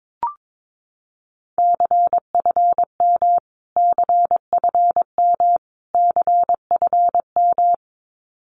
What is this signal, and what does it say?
Morse code, heard as a keyed tone, about 22 wpm, 700 Hz. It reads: CFM CFM CFM
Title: Abbreviations Repeated 3x - 22wpm